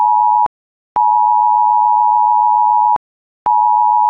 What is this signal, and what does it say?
Emergency Alert System (SAME) tones (the 853 and 960 Hz attention tone). It is heard from 0.9 to 3.0 s.
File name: Folder-Y110S / ui / emergency alert